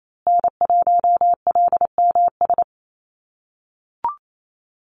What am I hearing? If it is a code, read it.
Morse code, heard as a keyed tone, about 28 wpm, 700 Hz. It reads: N1LMS
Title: USA Call Signs Repeated 2x 28wpm